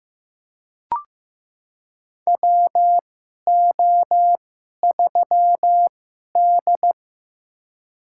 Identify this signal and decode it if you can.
Morse code, heard as a keyed tone, about 15 wpm, spaced as for 11 wpm, 700 Hz. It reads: WO3D
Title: USA Call Signs - 4 Characters - 2x Character Spacing - 15wpm